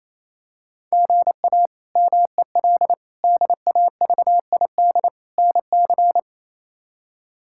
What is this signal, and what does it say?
Morse code, heard as a keyed tone, about 28 wpm, 700 Hz. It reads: GA MEL DAVID NC
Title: K1USN SST QSO Practice - 28wpm